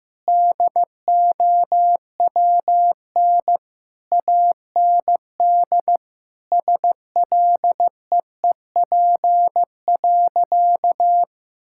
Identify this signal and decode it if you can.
Morse code, heard as a keyed tone, about 15 wpm, 700 Hz. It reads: DOWN AND SLEEP.
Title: Sentences from Top 1600 Words 15wpm